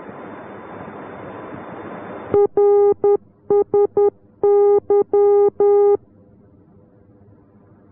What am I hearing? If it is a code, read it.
Morse code, heard as a keyed tone, about 10 wpm, 410 Hz. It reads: RSY